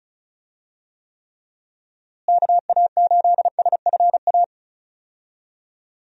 Morse code, heard as a keyed tone, about 36 wpm, 700 Hz. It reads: KA8SFA
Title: USA Call Signs - 6 Characters - Repeated 2x - 35wpm